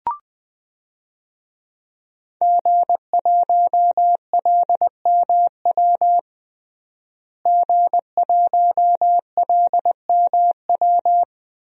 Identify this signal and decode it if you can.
Morse code, heard as a keyed tone, about 21 wpm, 700 Hz. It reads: G1LMW G1LMW
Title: DXCC-100 Call Signs - 5 Characters - Repeated 2x - 20wpm